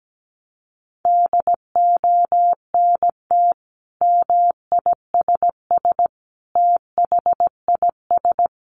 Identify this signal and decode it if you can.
Morse code, heard as a keyed tone, about 17 wpm, 700 Hz. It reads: DONT MISS THIS
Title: Sentences from Top 800 Words 17wpm